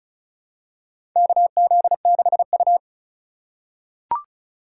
Morse code, heard as a keyed tone, about 36 wpm, 700 Hz. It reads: KZ6U
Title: USA Call Signs - 4 Characters - 3x Character Spacing - 35wpm